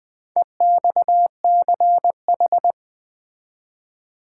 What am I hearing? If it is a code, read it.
Morse code, heard as a keyed tone, about 20 wpm, 700 Hz. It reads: EXCH
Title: Build A Word - 4 Letters At A Time - 8-Letter Words - 20wpm